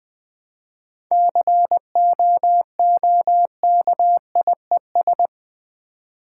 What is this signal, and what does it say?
Morse code, heard as a keyed tone, about 20 wpm, 700 Hz. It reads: COOKIES
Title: Easy Words with Hints - 20wpm